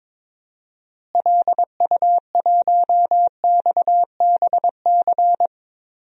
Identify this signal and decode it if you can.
Morse code, heard as a keyed tone, about 21 wpm, 700 Hz. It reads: LU1XBC
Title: DXCC-100 Call Signs - 6 Characters - Repeated 3x - 22wpm